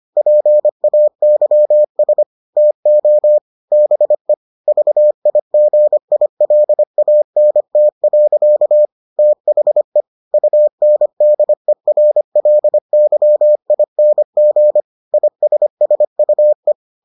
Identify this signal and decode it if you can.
Morse code, heard as a keyed tone, about 25 wpm, 600 Hz. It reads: PAYS TO BE VIGILANT. THE UNDERLYING ISSUE